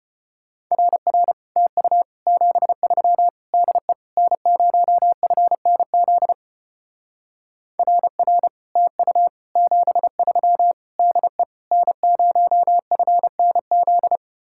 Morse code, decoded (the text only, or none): RR TU 73 DE N0FNZ RR TU 73 DE N0FNZ